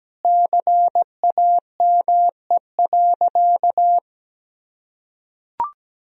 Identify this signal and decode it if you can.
Morse code, heard as a keyed tone, about 17 wpm, 700 Hz. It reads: CAME.
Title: Sentences from Top 1000 Words - Review Part 2 - 17wpm